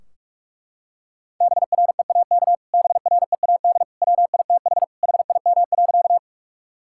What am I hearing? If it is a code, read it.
Morse code, heard as a keyed tone, about 42 wpm, 700 Hz. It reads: BREAK BREAD WITH HIM.
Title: Sentences from Top 1600 Words 45wpm